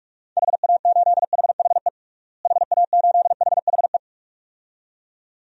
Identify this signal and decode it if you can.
Morse code, heard as a keyed tone, about 46 wpm, 700 Hz. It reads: HA8HHE HA8HHE
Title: DXCC-100 Call Signs - 6 Characters - Repeated 2x - 45wpm